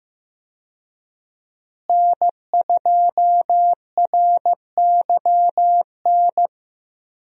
Morse code, heard as a keyed tone, about 15 wpm, 700 Hz. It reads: N2RYN
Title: USA Call Signs - 5 Characters - Repeated 3x - 15wpm